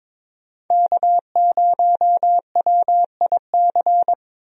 Morse code, heard as a keyed tone, about 22 wpm, 700 Hz. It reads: K0WIC
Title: USA Call Signs - 5 Characters - 22wpm